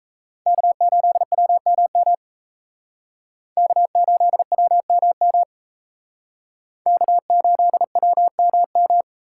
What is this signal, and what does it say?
Morse code, heard as a keyed tone, about 34 wpm, 700 Hz. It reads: K8WMM K8WMM K8WMM
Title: Speed Racing - USA 1x3 Call Signs 28wpm